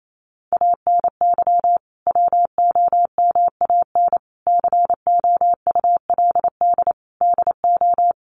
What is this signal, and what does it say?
Morse code, heard as a keyed tone, about 28 wpm, 700 Hz. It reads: ANY WOMAN COULD DO
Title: Sentences from Top 1100 Words 28wpm